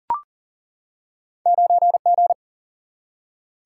Morse code, heard as a keyed tone, about 41 wpm, 700 Hz. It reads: 9G
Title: Single-Letter Single-Number 40wpm